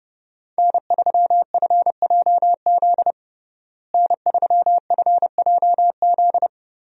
Morse code, heard as a keyed tone, about 30 wpm, 700 Hz. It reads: N3FJZ N3FJZ
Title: USA Call Signs - 5 Characters - Repeated 2x - 30wpm